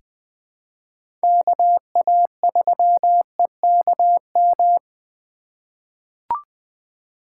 Morse code, heard as a keyed tone, about 20 wpm, 700 Hz. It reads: KA3EKM
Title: USA Call Signs - 4x Character Spacing - 20wpm